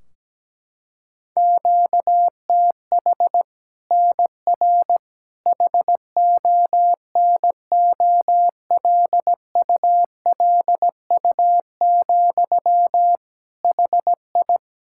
Morse code, heard as a keyed tone, about 17 wpm, 700 Hz. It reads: QTH NR HONOLULU, HI
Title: Practice QSO - Exchange 1 - Beginning - 17wpm